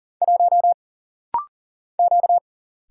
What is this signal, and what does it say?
Morse code, heard as a keyed tone, about 40 wpm, 700 Hz. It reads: 1 Q
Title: Single Letter-Number Alt - Rapid-Fire 40wpm